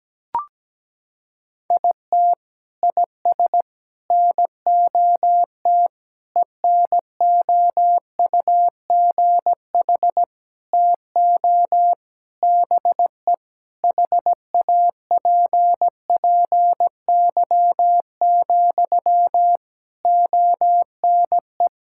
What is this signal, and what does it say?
Morse code, heard as a keyed tone, about 17 wpm, 700 Hz. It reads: IT IS NOT ENOUGH TO BE HAPPY, ONE
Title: Sentences from Top 1000 Words - Review Part 6 - 17wpm